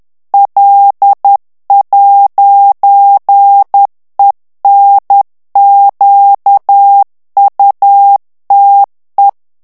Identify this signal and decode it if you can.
Morse code, heard as a keyed tone, about 10 wpm, 800 Hz. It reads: L'ENQUTE
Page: Français - texte en clair 1 - vitesse : 10 mots minute